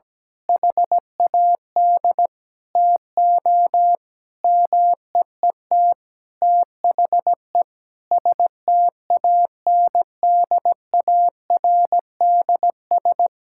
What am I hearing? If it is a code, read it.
Morse code, heard as a keyed tone, about 17 wpm, 700 Hz. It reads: HAD TO MEET THE STANDARDS
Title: Sentences from Top 1000 Words - Review Part 6 - 17wpm